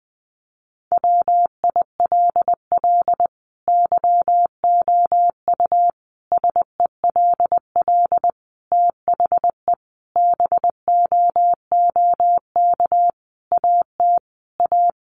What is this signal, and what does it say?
Morse code, heard as a keyed tone, about 19 wpm, 700 Hz. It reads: WILL YOU SELL THE BOOK AT A
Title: Sentences from Top 1200 Words 20wpm